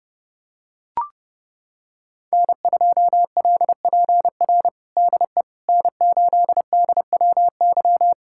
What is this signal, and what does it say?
Morse code, heard as a keyed tone, about 30 wpm, 700 Hz. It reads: N2LPR DE N8DWY
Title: USA Callsign DE Callsign - 5 Characters - 30wpm